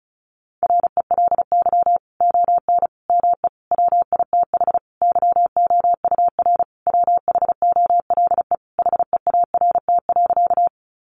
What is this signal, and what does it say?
Morse code, heard as a keyed tone, about 34 wpm, 700 Hz. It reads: RELY ON ME WITH YOUR WHOLE HEART.
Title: Sentences from Top 1200 Words 35wpm